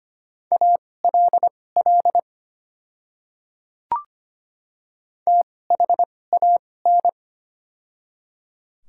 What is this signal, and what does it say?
Morse code, heard as a keyed tone, about 25 wpm, spaced as for 18 wpm, 700 Hz. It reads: ALL THAN
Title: 2x Character Spacing - Top 100 Words - 25wpm